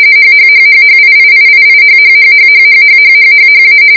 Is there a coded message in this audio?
Yes: RTTY from 0.2 to 3.8 s, about 45.45 baud. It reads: HHHHHIIIIIHHHHHHHHHHHA